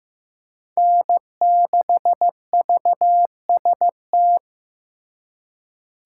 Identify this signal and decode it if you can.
Morse code, heard as a keyed tone, about 18 wpm, 700 Hz. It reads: N6VST
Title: Speed Racing - USA 1x3 Call Signs 15wpm